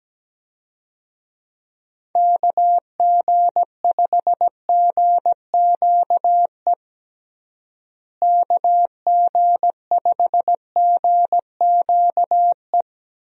Morse code, heard as a keyed tone, about 17 wpm, 700 Hz. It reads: KG5GQE KG5GQE
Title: USA Call Signs - 6 Characters - Repeated 3x - 17wpm